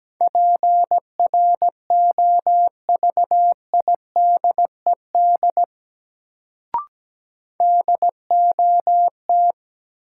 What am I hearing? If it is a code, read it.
Morse code, heard as a keyed tone, about 17 wpm, 700 Hz. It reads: PROVIDED DOT
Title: Sentences from Top 1500 Words - New Word Drill Alt - 17wpm